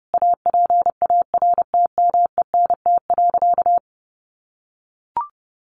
Morse code, heard as a keyed tone, about 28 wpm, 700 Hz. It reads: APARTMENT.
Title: Sentences from Top 900 Words 30wpm